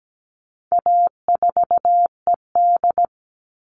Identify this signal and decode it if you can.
Morse code, heard as a keyed tone, about 17 wpm, 700 Hz. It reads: A4ED
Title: All Cut Numbers - 4-Digit 17wpm